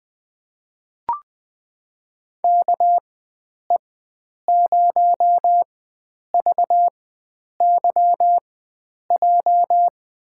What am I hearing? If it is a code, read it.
Morse code, heard as a keyed tone, about 20 wpm, spaced as for 9 wpm, 700 Hz. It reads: KE0VYJ
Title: USA Call Signs - 4x Character Spacing - 20wpm